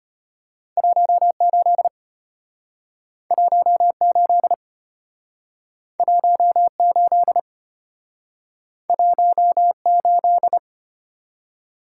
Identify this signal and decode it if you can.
Morse code, heard as a keyed tone, about 34 wpm, 700 Hz. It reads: 18 18 18 18